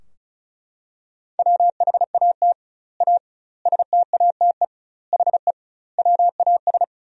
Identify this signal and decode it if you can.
Morse code, heard as a keyed tone, about 36 wpm, 700 Hz. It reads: WHAT A STATE HE WAS
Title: 2x Word-Spacing - Sentences from Top 400 Words - 35wpm